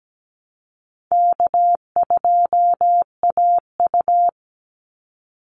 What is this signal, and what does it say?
Morse code, heard as a keyed tone, about 17 wpm, 700 Hz. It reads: K2AU
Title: Call Signs Repeated 3x - 17wpm